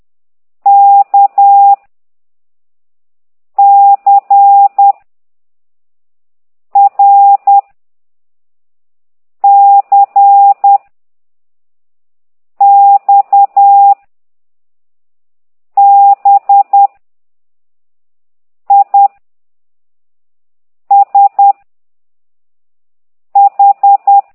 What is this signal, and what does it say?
Morse code, heard as a keyed tone, about 10 wpm, 800 Hz. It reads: K C R C X B I S H